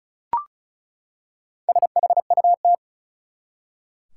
Morse code, heard as a keyed tone, about 36 wpm, 700 Hz. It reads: SHUT
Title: Four Letter Words - 35wpm